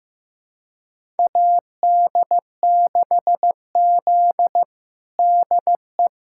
Morse code, heard as a keyed tone, about 15 wpm, 700 Hz. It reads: AD6Z DE
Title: USA Callsign DE Callsign - 4 Characters - 15wpm